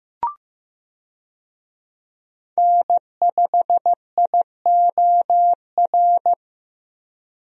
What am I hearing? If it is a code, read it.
Morse code, heard as a keyed tone, about 15 wpm, 700 Hz. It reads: N5IOR
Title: USA Call Signs - 5 Characters - Repeated 3x - 15wpm